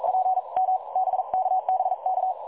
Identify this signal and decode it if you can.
Morse code, heard as a keyed tone, about 46 wpm, spaced as for 36 wpm, 750 Hz. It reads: PM BK BK
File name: dlazaawansowanych / nagraniazpasma / websdr recording 3565.0kHz